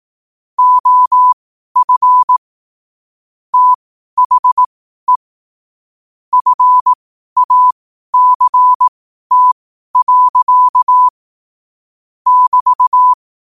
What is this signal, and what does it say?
Morse code, heard as a keyed tone, about 18 wpm, spaced as for 12 wpm, 1000 Hz. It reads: OF THE FACT. <BT>
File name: qotdfiles / 12wpm